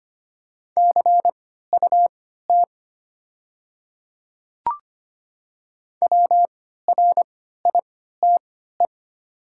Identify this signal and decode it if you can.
Morse code, heard as a keyed tone, about 25 wpm, spaced as for 14 wpm, 700 Hz. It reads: CUT WRITE
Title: 3x Character Spacing - Top 300 Words - 25wpm